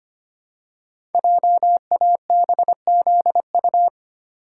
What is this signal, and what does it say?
Morse code, heard as a keyed tone, about 25 wpm, 700 Hz. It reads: JABZU
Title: Code Groups - 5 Letters - 25wpm